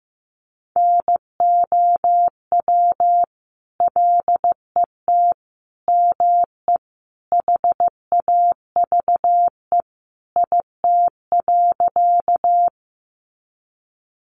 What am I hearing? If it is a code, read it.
Morse code, heard as a keyed tone, about 15 wpm, 700 Hz. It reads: NOW LET ME HAVE IT.
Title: Sentences from Top 300 Words 15wpm